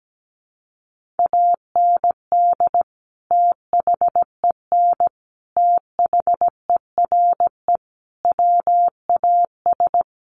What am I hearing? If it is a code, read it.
Morse code, heard as a keyed tone, about 17 wpm, 700 Hz. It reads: AND THEN THERE WAS
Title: Sentences from Top 1400 Words 17wpm